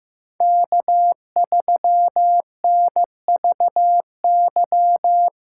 Morse code, heard as a keyed tone, about 15 wpm, 700 Hz. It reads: K3NVY
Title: USA Callsign DE Callsign - 5 Characters - 15wpm